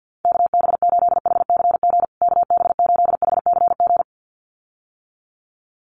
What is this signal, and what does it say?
Morse code, heard as a keyed tone, about 50 wpm, 700 Hz. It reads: KB8HCG KB8HCG
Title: USA Call Signs Repeated 2x 50wpm